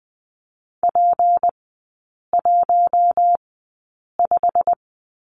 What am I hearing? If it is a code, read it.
Morse code, heard as a keyed tone, about 20 wpm, spaced as for 8 wpm, 700 Hz. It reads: P15
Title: 152 Lesson 25e - Letter p - Three Characters 20wpm